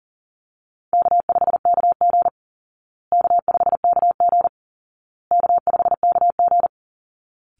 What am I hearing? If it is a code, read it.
Morse code, heard as a keyed tone, about 40 wpm, 700 Hz. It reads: K5KG K5KG K5KG
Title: Easy Call Signs Repeated 3x - 40wpm